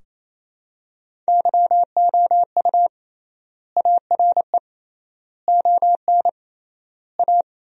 Morse code, heard as a keyed tone, about 28 wpm, 700 Hz. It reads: YOU ARE ON A
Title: 3x Word-Spacing - Sentences from Top 400 Words - 28wpm